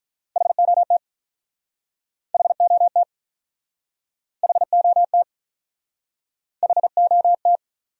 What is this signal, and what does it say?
Morse code, heard as a keyed tone, about 47 wpm, 700 Hz. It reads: HOT HOT HOT HOT